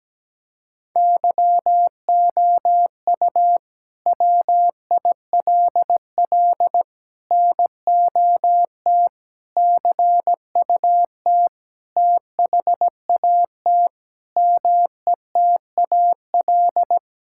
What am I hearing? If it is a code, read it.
Morse code, heard as a keyed tone, about 17 wpm, 700 Hz. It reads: YOU WILL NOT CUT THAT METAL
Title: Sentences from Top 1000 Words - Review Part 6 - 17wpm